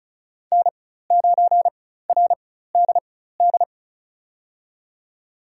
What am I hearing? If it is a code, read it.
Morse code, heard as a keyed tone, about 36 wpm, spaced as for 16 wpm, 700 Hz. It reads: N9RDD
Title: USA Call Signs - 5 Characters - 4x Character Spacing - 35wpm